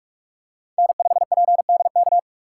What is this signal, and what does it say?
Morse code, heard as a keyed tone, about 43 wpm, 700 Hz. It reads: N5PDK
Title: USA Call Signs - 5 Characters - 3x Character Spacing - 45wpm